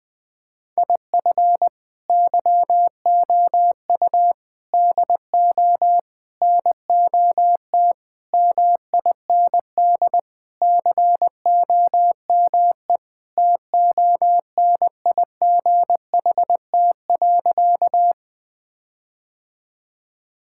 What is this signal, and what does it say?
Morse code, heard as a keyed tone, about 21 wpm, 690 Hz. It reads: IF YOU DO NOT MIND COME TONIGHT.
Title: Sentences with 4-Word Phrases - Spoken Beginning - 20wpm